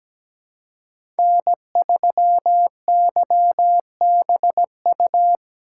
Morse code, heard as a keyed tone, about 17 wpm, 700 Hz. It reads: N3YBU
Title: USA Call Signs - 5 Characters - 6x Character Spacing - 17wpm